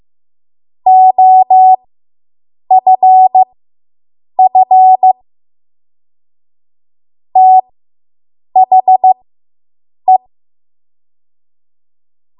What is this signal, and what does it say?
Morse code, heard as a keyed tone, about 15 wpm, spaced as for 7 wpm, 750 Hz. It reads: OFF THE